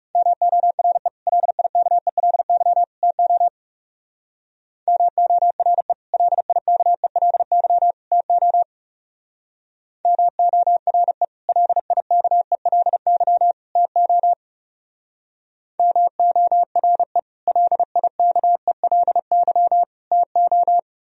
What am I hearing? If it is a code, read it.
Morse code, heard as a keyed tone, about 36 wpm, 700 Hz. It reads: MORE LIKELY TO MORE LIKELY TO MORE LIKELY TO MORE LIKELY TO
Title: Speed-Racing - Sets of 3 Words 30wpm